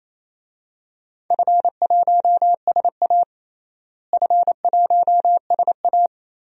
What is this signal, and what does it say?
Morse code, heard as a keyed tone, about 28 wpm, 700 Hz. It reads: F1SA F1SA
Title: DXCC-100 Call Signs - 4 Characters - Repeated 2x - 28wpm